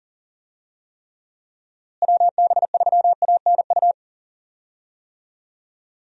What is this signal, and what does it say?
Morse code, heard as a keyed tone, about 40 wpm, 700 Hz. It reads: WB3ANU
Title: USA Call Signs - 6 Characters - Repeated 2x - 40wpm